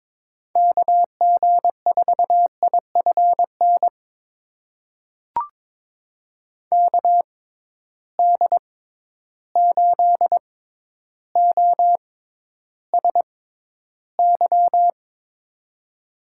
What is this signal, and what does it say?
Morse code, heard as a keyed tone, about 22 wpm, spaced as for 11 wpm, 700 Hz. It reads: KG4IFN KD8OSY
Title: USA Call Signs - 6 Characters - 6x Character Spacing - 22wpm